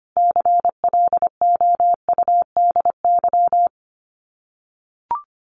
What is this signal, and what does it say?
Morse code, heard as a keyed tone, about 25 wpm, 700 Hz. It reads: CLOUDY
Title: Common QSO Phrases 25wpm